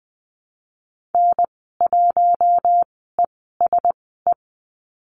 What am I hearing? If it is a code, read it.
Morse code, heard as a keyed tone, about 20 wpm, spaced as for 14 wpm, 700 Hz. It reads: N1ESE
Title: 042 Lesson 8g - Number 1 - Call Signs 20wpm